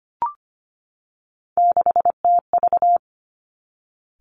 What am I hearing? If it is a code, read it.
Morse code, heard as a keyed tone, about 25 wpm, 700 Hz. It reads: 6TV
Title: Three Challenging Letter-Number Combinations 25wpm